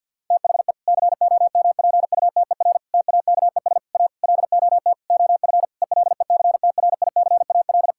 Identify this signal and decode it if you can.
Morse code, heard as a keyed tone, about 50 wpm, 700 Hz. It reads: THE COMPUTER TAKES A LOT OF ELECTRICAL
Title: Sentences from Top 1000 Words - Review Part 6 - 50wpm